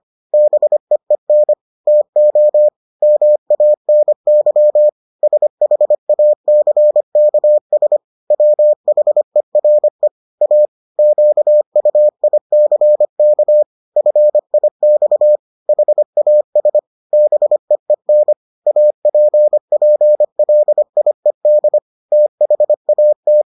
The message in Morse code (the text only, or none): BEEN TO MANY SHACKS WHERE A QUICK FIX HAS BEEN APPLIED THAT